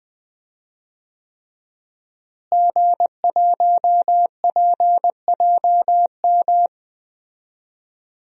Morse code, heard as a keyed tone, about 20 wpm, 700 Hz. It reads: G1PJM
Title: DXCC-100 Call Signs - 5 Characters - Repeated 3x - 20wpm